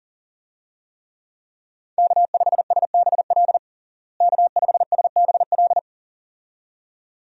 Morse code, heard as a keyed tone, about 40 wpm, 700 Hz. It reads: K5SBL K5SBL
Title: USA Call Signs - 5 Characters - Repeated 2x - 40wpm